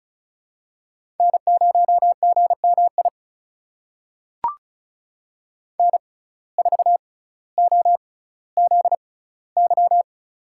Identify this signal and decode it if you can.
Morse code, heard as a keyed tone, about 36 wpm, spaced as for 18 wpm, 700 Hz. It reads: N0GMI N4OZY
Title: USA Call Signs - 5 Characters - 6x Character Spacing - 35wpm